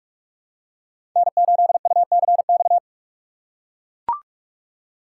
Morse code, heard as a keyed tone, about 43 wpm, 700 Hz. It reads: N8UCX
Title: USA Call Signs - 5 Characters - 4x Character Spacing - 45wpm